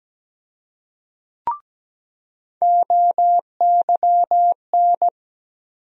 Morse code, heard as a keyed tone, about 17 wpm, 700 Hz. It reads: OYN
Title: USA Call Sign Postfixes - 3 Letters 17wpm